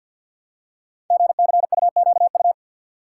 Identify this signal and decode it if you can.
Morse code, heard as a keyed tone, about 47 wpm, 700 Hz. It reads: GCUQU
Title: Code Groups - 5 Letters - 50wpm